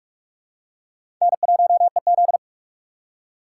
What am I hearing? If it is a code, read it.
Morse code, heard as a keyed tone, about 46 wpm, 700 Hz. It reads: N1EZ
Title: USA Call Signs - 4 Characters - 6x Character Spacing - 45wpm